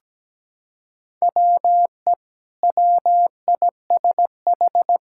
Morse code, heard as a keyed tone, about 17 wpm, 700 Hz. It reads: WE WISH
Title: Sentences from Top 1600 Words 17wpm